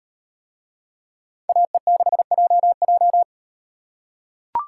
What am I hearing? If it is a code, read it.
Morse code, heard as a keyed tone, about 38 wpm, 700 Hz. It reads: AE6JJ